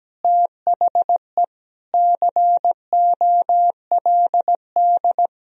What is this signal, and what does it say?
Morse code, heard as a keyed tone, about 17 wpm, 700 Hz. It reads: THE COLD
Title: Sentences from Top 1000 Words - Review Part 3 - 17wpm